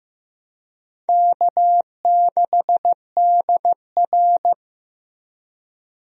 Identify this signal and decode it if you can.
Morse code, heard as a keyed tone, about 15 wpm, 700 Hz. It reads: K6DR
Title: USA Call Signs - 4 Characters - Repeated 2x - 15wpm